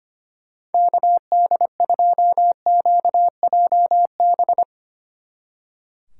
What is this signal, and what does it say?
Morse code, heard as a keyed tone, about 33 wpm, 700 Hz. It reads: KD2QJB